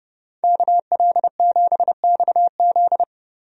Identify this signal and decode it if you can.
Morse code, heard as a keyed tone, about 30 wpm, 700 Hz. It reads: KL7XZ
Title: DXCC-100 Call Signs - 5 Characters - Repeated 3x - 30wpm